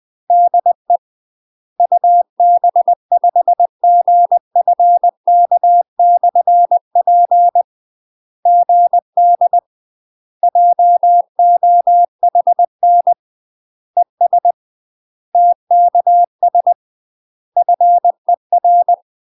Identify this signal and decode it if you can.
Morse code, heard as a keyed tone, about 20 wpm, 700 Hz. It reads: DE UB5GFK/P GD JOHN ES TKS FER
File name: UK MASTER QSOs TRAINING / 20WPM / UK MASTER QSOs 0007